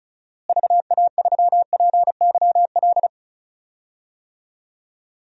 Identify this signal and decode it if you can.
Morse code, heard as a keyed tone, about 36 wpm, 700 Hz. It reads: VA3PYL
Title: DXCC-100 Call Signs - 6 Characters - Repeated 3x - 35wpm